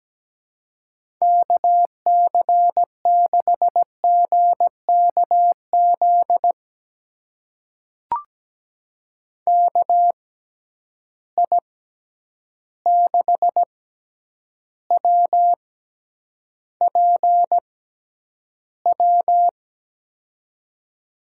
Morse code, heard as a keyed tone, about 17 wpm, spaced as for 9 wpm, 700 Hz. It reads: KC6GKZ KI6WPW
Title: USA Call Signs - 6 Characters - 6x Character Spacing - 17wpm